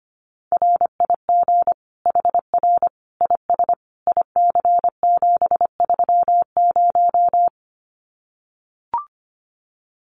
Morse code, heard as a keyed tone, about 25 wpm, 700 Hz. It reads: RIG HR IS IC730
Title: Common QSO Phrases 25wpm